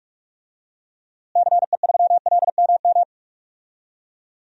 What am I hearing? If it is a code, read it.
Morse code, heard as a keyed tone, about 46 wpm, 700 Hz. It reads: CE3LMM
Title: DXCC-100 Call Signs - 6 Characters - Repeated 3x - 45wpm